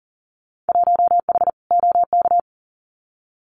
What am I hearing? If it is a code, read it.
Morse code, heard as a keyed tone, about 40 wpm, 700 Hz. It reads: 1H OK
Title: Winter Field Day Exchanges 40wpm